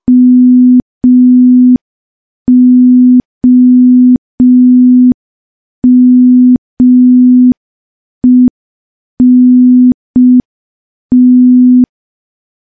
Morse code, heard as a keyed tone, about 5 wpm, 250 Hz. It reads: MOMENT
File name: morse / morse-training-05WPM